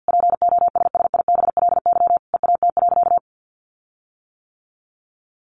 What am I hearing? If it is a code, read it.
Morse code, heard as a keyed tone, about 50 wpm, 700 Hz. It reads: POSSIBLY EAT.
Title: Sentences from Top 1300 Words 50wpm